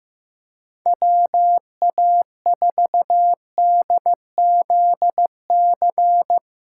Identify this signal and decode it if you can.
Morse code, heard as a keyed tone, about 15 wpm, 700 Hz. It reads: WA4DZC
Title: USA Callsign DE Callsign - 6 Characters - 15wpm